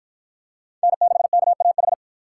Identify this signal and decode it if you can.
Morse code, heard as a keyed tone, about 48 wpm, 700 Hz. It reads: N6KAH